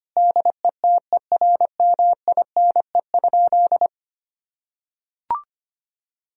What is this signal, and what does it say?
Morse code, heard as a keyed tone, about 25 wpm, 700 Hz. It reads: DETERMINE?
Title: Sentences from Top 1000 Words - Review Part 4 - 25wpm